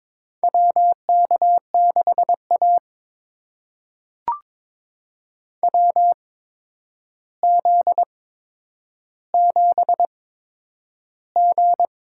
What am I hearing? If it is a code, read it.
Morse code, heard as a keyed tone, about 22 wpm, spaced as for 9 wpm, 700 Hz. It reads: WK6A WZ7G
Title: USA Call Signs - 4 Characters - 8x Character Spacing - 22wpm